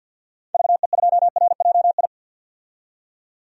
Morse code, heard as a keyed tone, about 51 wpm, 700 Hz. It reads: VE2RJI
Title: DXCC-100 Call Signs - 6 Characters - 50wpm